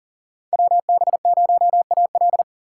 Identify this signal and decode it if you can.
Morse code, heard as a keyed tone, about 40 wpm, 700 Hz. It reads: WB0AL
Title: USA Call Signs - 5 Characters - 4x Character Spacing - 40wpm